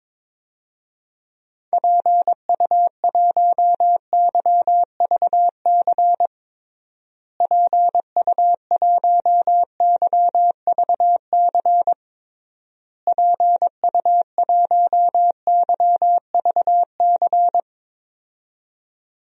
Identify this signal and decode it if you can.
Morse code, heard as a keyed tone, about 21 wpm, 700 Hz. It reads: PU1YVC PU1YVC PU1YVC
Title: DXCC-100 Call Signs - 6 Characters - Repeated 3x - 22wpm